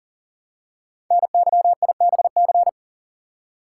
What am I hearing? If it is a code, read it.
Morse code, heard as a keyed tone, about 43 wpm, 690 Hz. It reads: NYIBC
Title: Code Groups - 5 Letter:Numbers - 40wpm